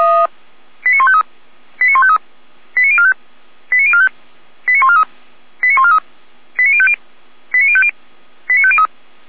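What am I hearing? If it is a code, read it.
Five-tone selective call (ZVEI), heard as quick runs of five tones, tones of about 70 ms. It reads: 89153 89153 89045 89045 89143 89143 89050 89050 89503
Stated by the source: ZVEI five-tone selective call received by SDR, logged by feuerwehr-orlamuende (fire brigade Orlamünde).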